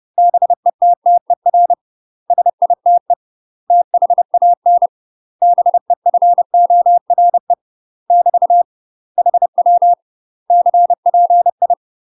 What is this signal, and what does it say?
Morse code, heard as a keyed tone, about 30 wpm, 700 Hz. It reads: BETTER SITE THAN BEFORE <BT> HW CPI
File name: UK MASTER QSOs TRAINING / 30WPM / UK MASTER QSOs 0035